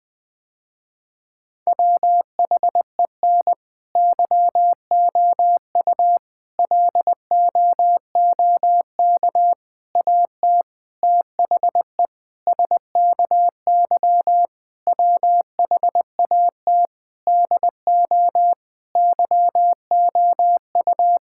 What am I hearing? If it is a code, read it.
Morse code, heard as a keyed tone, about 20 wpm, 700 Hz. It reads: WHEN YOU LOOK AT THE SKY WHAT DO YOU
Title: Sentences with 3-Word Phrases - Spoken Beginning - 20wpm